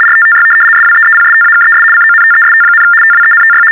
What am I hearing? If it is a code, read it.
RTTY, about 75 baud. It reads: RTTY SWEEPSTAKES CONTEST, TWENTY
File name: teletype-RTTY-75